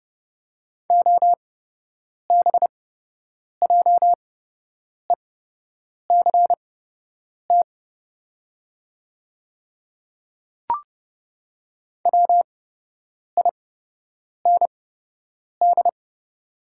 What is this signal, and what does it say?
Morse code, heard as a keyed tone, about 30 wpm, spaced as for 8 wpm, 700 Hz. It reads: OBJECT WIND
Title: 8x Character Spacing - Top 500 Words - 30wpm